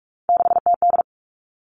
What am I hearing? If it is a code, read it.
Morse code, heard as a keyed tone, about 46 wpm, 700 Hz. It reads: 6TD
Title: Three Challenging Letter-Number Combinations 45wpm